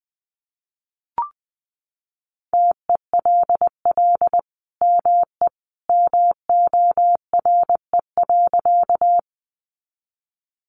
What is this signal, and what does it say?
Morse code, heard as a keyed tone, about 20 wpm, 700 Hz. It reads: TELL ME MORE.
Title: Sentences from Top 300 Words 20wpm